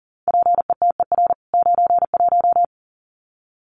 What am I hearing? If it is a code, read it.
Morse code, heard as a keyed tone, about 40 wpm, 700 Hz. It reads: PETER 91
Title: CWT Exchange 40wpm